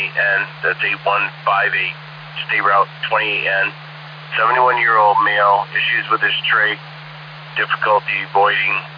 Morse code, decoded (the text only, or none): none